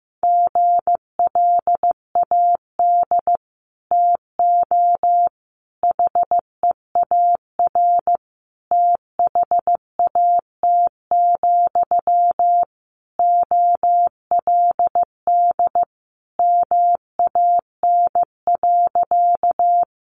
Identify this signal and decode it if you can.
Morse code, heard as a keyed tone, about 15 wpm, 700 Hz. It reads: GLAD TO HEAR THAT, OLD MAN.
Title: Sentences from Top 800 Words 15wpm